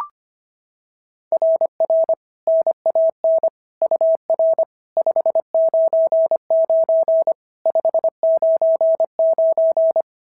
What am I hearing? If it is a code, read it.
Morse code, heard as a keyed tone, about 25 wpm, 690 Hz. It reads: RR NAN UR 599 599
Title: SOTA QSO Practice 25wpm